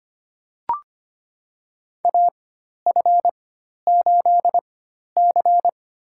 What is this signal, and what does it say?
Morse code, heard as a keyed tone, about 25 wpm, spaced as for 12 wpm, 700 Hz. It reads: AF8C
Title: USA Call Signs - 4 Characters - 4x Character Spacing - 25wpm